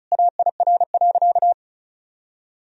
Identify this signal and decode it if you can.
Morse code, heard as a keyed tone, about 34 wpm, 700 Hz. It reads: AIR.
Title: Sentences from Top 1000 Words - Review Part 1 - 35wpm